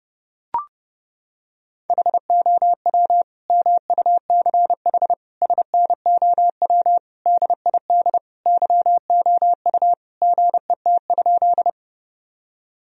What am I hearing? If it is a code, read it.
Morse code, heard as a keyed tone, about 30 wpm, 700 Hz. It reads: HOW MUCH SNOW DID YOU GET?
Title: Sentences from Top 1000 Words - Review Part 3 - 30wpm